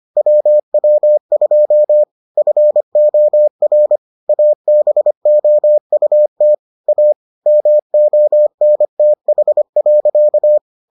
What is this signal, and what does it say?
Morse code, heard as a keyed tone, about 25 wpm, 600 Hz. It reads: WW2 FOR ABOUT A MONTH.